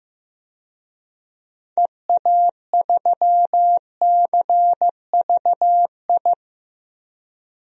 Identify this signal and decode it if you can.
Morse code, heard as a keyed tone, about 16 wpm, 700 Hz. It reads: EA3CVI
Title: DXCC-100 Call Signs - 6 Characters - 15wpm